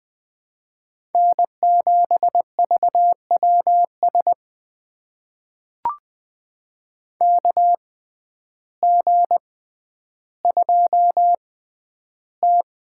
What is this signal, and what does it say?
Morse code, heard as a keyed tone, about 20 wpm, spaced as for 11 wpm, 700 Hz. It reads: N7VWS KG2T
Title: USA Call Signs - 6x Character Spacing - 20wpm